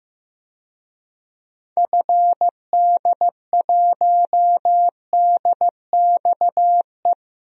Morse code, heard as a keyed tone, about 15 wpm, 700 Hz. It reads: FD1DXE
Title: DXCC-100 Call Signs - 6 Characters - Repeated 3x - 15wpm